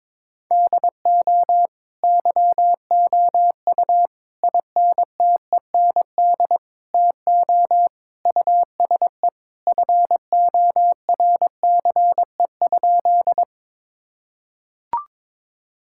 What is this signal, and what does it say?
Morse code, heard as a keyed tone, about 22 wpm, 700 Hz. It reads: DO YOU INTEND TO USE FORCE?
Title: Sentences from Top 1600 Words 22wpm